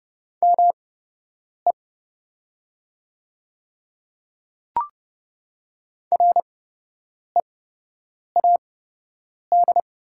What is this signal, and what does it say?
Morse code, heard as a keyed tone, about 30 wpm, spaced as for 8 wpm, 700 Hz. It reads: ME READ
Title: 8x Character Spacing - Top 300 Words - 30wpm